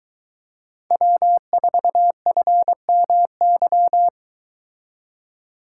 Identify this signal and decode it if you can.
Morse code, heard as a keyed tone, about 22 wpm, 700 Hz. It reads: W4FMY